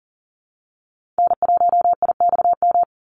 Morse code, heard as a keyed tone, about 40 wpm, 700 Hz. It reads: N1IXM
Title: USA Call Signs 40wpm